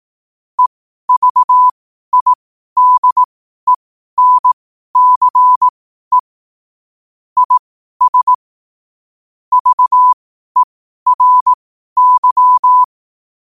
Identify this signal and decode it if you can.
Morse code, heard as a keyed tone, about 18 wpm, spaced as for 12 wpm, 1000 Hz. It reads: EVIDENCE IS VERY
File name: qotdfiles / 12wpm